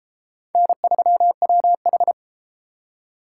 Morse code, heard as a keyed tone, about 32 wpm, 700 Hz. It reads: N3WH